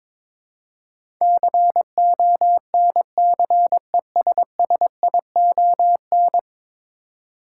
Morse code, heard as a keyed tone, about 22 wpm, 700 Hz. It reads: CONCESSION
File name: practice / 22wpm / 10 Letter Words - First 3 Letters Given - 22wpm